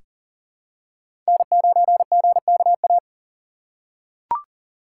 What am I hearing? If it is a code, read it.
Morse code, heard as a keyed tone, about 40 wpm, 700 Hz. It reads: N9GKA
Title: USA Call Signs - 4x Character Spacing - 40wpm